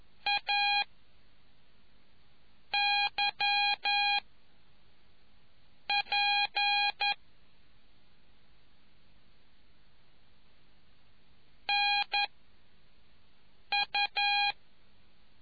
Morse code, heard as a keyed tone, about 11 wpm, 820 Hz. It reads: A Y P N U